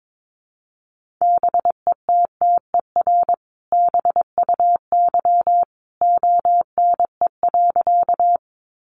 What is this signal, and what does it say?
Morse code, heard as a keyed tone, about 22 wpm, 700 Hz. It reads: BETTER BUY ONE.
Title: Sentences from Top 700 Words 22wpm